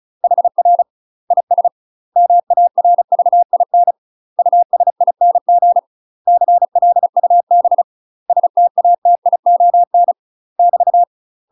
Morse code, heard as a keyed tone, about 36 wpm, 700 Hz. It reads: HR IS MARVIN USING CLUB STATION <BT>
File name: UK MASTER QSOs TRAINING / 35WPM / UK MASTER QSOs 0014